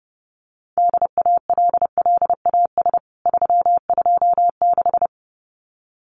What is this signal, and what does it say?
Morse code, heard as a keyed tone, about 30 wpm, 700 Hz. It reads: DALLAS 326
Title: CWT Exchange 30wpm